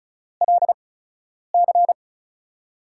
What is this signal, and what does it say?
Morse code, heard as a keyed tone, about 36 wpm, spaced as for 10 wpm, 700 Hz. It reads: LC
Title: USA Call Signs - 4 Characters - 8x Character Spacing - 35wpm